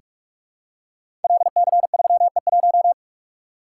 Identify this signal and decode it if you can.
Morse code, heard as a keyed tone, about 44 wpm, 700 Hz. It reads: LC3E1
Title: Code Groups - 5 Letter:Numbers - 45wpm